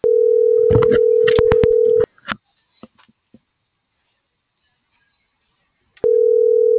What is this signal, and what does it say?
Ringback tone (North America), 440 + 480 Hz.